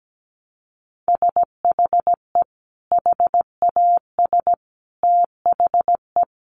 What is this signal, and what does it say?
Morse code, heard as a keyed tone, about 17 wpm, 700 Hz. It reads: SHE HAS THE
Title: Sentences from Top 800 Words 17wpm